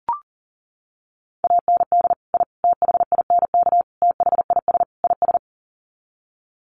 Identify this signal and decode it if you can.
Morse code, heard as a keyed tone, about 40 wpm, 700 Hz. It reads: AND I THINK THIS IS
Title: Sets of 5 Words Encore 40wpm